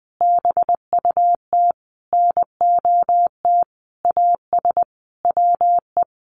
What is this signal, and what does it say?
Morse code, heard as a keyed tone, about 20 wpm, 700 Hz. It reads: BUT NOT AS WE
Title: Sentences from Top 1100 Words 20wpm